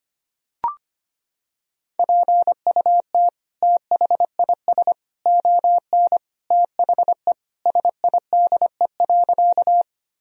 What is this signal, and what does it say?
Morse code, heard as a keyed tone, about 25 wpm, 700 Hz. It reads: PUT THIS ON THE SIDE.
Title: Sentences from Top 1000 Words - Review Part 1 - 25wpm